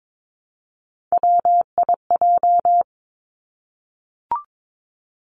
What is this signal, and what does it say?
Morse code, heard as a keyed tone, about 22 wpm, 700 Hz. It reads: WIJ
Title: Three Challenging Letter-Number Combinations 22wpm